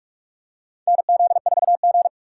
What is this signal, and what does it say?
Morse code, heard as a keyed tone, about 46 wpm, 700 Hz. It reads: NZ4G
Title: USA Call Signs - 4 Characters - 3x Character Spacing - 45wpm